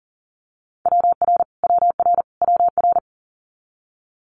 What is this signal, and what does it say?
Morse code, heard as a keyed tone, about 40 wpm, 700 Hz. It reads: WR WR WR
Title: Multi-letter Phonemes Repeated 3x 40wpm